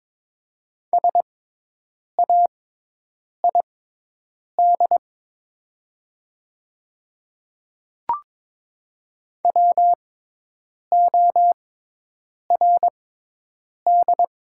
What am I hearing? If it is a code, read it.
Morse code, heard as a keyed tone, about 22 wpm, spaced as for 8 wpm, 700 Hz. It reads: SAID WORD
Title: 6x Character Spacing - Top 100 Words - 22wpm